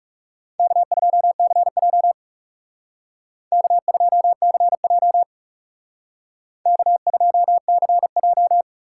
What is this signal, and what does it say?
Morse code, heard as a keyed tone, about 36 wpm, 700 Hz. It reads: K2CJ K2CJ K2CJ
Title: Speed Racing - USA 1x2 Call Signs 30wpm